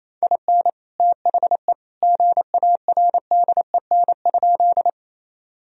Morse code, heard as a keyed tone, about 27 wpm, 700 Hz. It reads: IN THE GARDEN?
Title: Sentences from Top 1000 Words - Review Part 4 - 28wpm